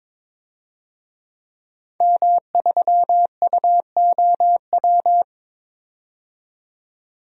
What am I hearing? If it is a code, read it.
Morse code, heard as a keyed tone, about 22 wpm, 700 Hz. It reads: M3UOW
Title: DXCC-100 Call Signs - 5 Characters - 22wpm